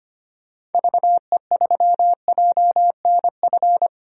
Morse code, heard as a keyed tone, about 25 wpm, 700 Hz. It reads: VE3JNF
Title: DXCC-100 Call Signs - 6 Characters - 25wpm